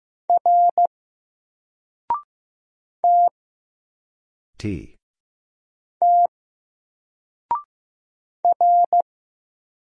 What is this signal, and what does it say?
Morse code, heard as a keyed tone, about 15 wpm, 700 Hz. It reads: R T T R
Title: Single Letter-Number Alt 15wpm